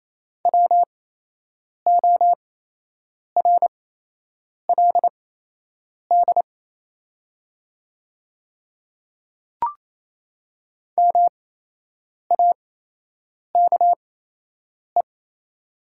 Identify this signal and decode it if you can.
Morse code, heard as a keyed tone, about 28 wpm, spaced as for 8 wpm, 700 Hz. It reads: WORLD MAKE
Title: 8x Character Spacing - Top 500 Words - 28wpm